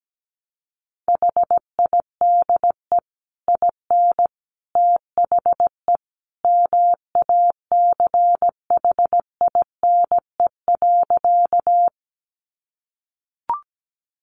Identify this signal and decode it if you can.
Morse code, heard as a keyed tone, about 17 wpm, 700 Hz. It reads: HIDE IN THE MACHINE.
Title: A Sound Of Thunder 17wpm